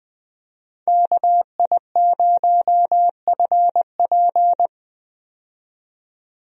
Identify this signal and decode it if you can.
Morse code, heard as a keyed tone, about 20 wpm, 700 Hz. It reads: KI0FP
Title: USA Call Signs - 5 Characters - 20wpm